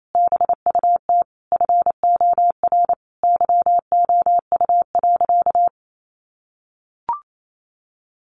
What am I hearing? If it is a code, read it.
Morse code, heard as a keyed tone, about 28 wpm, 700 Hz. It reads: BUT FOR YOU.
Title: Sentences from Top 1300 Words 28wpm